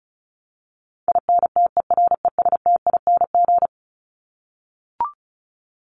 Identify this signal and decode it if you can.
Morse code, heard as a keyed tone, about 34 wpm, 700 Hz. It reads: INTERESTING
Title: ING Suffix - Top 100 Words with Base Words 35wpm